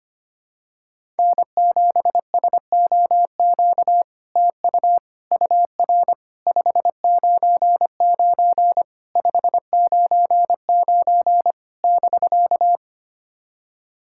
Morse code, heard as a keyed tone, about 25 wpm, 700 Hz. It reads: N7SOQ TU UR 599 599 <BK>
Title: SOTA QSO Practice 25wpm